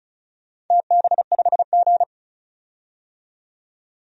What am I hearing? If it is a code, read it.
Morse code, heard as a keyed tone, about 36 wpm, 700 Hz. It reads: TB5G
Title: DXCC-100 Call Signs - 4 Characters - 35wpm